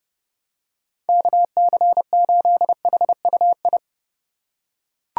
Morse code, heard as a keyed tone, about 30 wpm, 700 Hz. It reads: KC8HUI